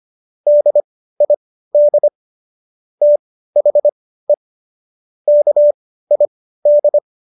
Morse code, heard as a keyed tone, about 25 wpm, spaced as for 15 wpm, 590 Hz. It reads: DID THE KID
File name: Jokes-25-char-15-eff